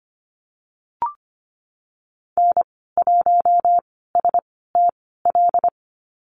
Morse code, heard as a keyed tone, about 25 wpm, spaced as for 16 wpm, 700 Hz. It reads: N1STL
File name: Lesson 15g - Number 5 - Call Signs 25wpm